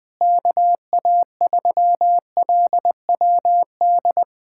Morse code, heard as a keyed tone, about 20 wpm, 700 Hz. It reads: KA3LWD
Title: USA Callsign DE Callsign 20wpm